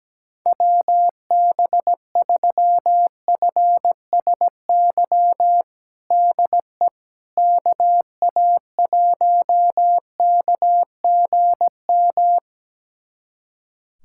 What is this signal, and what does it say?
Morse code, heard as a keyed tone, about 17 wpm, 700 Hz. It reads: WB3FSY DE KA1KGM
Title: USA Callsign DE Callsign - 6 Characters - 17wpm